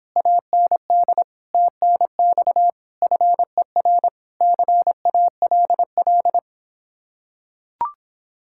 Morse code, heard as a keyed tone, about 26 wpm, 700 Hz. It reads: AND TNX FER CALL